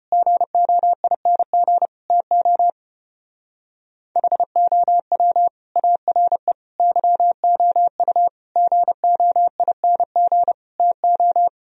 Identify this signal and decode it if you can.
Morse code, heard as a keyed tone, about 33 wpm, 700 Hz. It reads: GOING TO HOW ARE YOU GOING TO